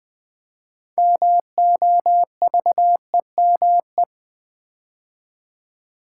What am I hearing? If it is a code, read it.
Morse code, heard as a keyed tone, about 20 wpm, 700 Hz. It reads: MOVEME
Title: Build A Word - 3 Letters At A Time - 8-Letter Words - 20wpm